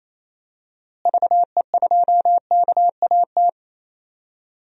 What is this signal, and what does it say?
Morse code, heard as a keyed tone, about 28 wpm, 700 Hz. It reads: VE2KAT
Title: DXCC-100 Call Signs - 6 Characters - Repeated 2x - 28wpm